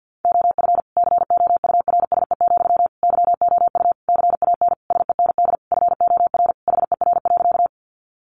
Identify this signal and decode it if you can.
Morse code, heard as a keyed tone, about 50 wpm, 700 Hz. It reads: OF COURSE, YOU CAN SEND FOR HER.
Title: Sentences from Top 1500 Words 50wpm